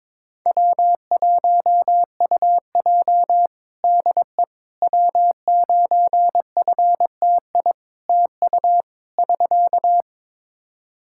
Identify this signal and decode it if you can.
Morse code, heard as a keyed tone, about 22 wpm, 700 Hz. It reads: W1UJ DE W9FTI TU <SK>
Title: Practice QSO - Exchange 4 - Ending - 22wpm